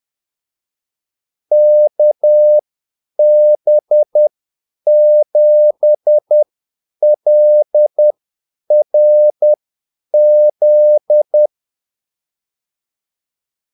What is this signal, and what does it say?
Morse code, heard as a keyed tone, about 10 wpm, spaced as for 8 wpm, 600 Hz. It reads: KB7LRZ
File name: Callsigns-10-8wpm